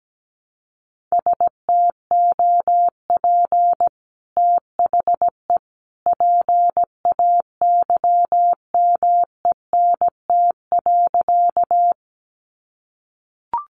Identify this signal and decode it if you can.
Morse code, heard as a keyed tone, about 17 wpm, 700 Hz. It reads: STOP THE PAYMENT.
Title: Sentences from Top 1200 Words 17wpm